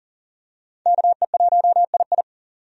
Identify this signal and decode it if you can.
Morse code, heard as a keyed tone, about 40 wpm, 700 Hz. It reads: KE1II
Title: USA Call Signs - 5 Characters - 40wpm